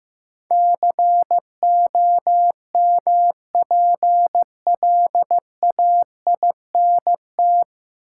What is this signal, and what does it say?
Morse code, heard as a keyed tone, about 15 wpm, 700 Hz. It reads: COMPLAINT
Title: Sentences from Top 1400 Words - New Word Drill Alt - 15wpm